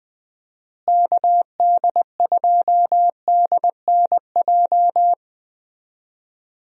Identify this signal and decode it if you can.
Morse code, heard as a keyed tone, about 21 wpm, 700 Hz. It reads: KD2DNJ
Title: USA Call Signs - 6 Characters - Repeated 2x - 20wpm